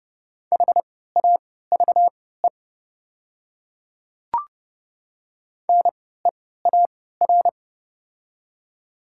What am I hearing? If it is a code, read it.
Morse code, heard as a keyed tone, about 30 wpm, spaced as for 17 wpm, 700 Hz. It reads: HAVE NEAR
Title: 3x Character Spacing - Top 300 Words - 30wpm